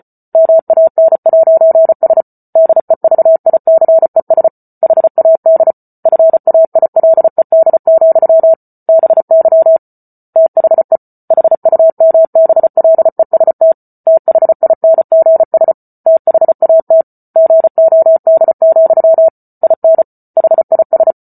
Morse code, heard as a keyed tone, about 34 wpm, 650 Hz. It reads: MAN'S DEVICES HAD FAILED, BY THE HUMBLEST THINGS THAT GOD, IN HIS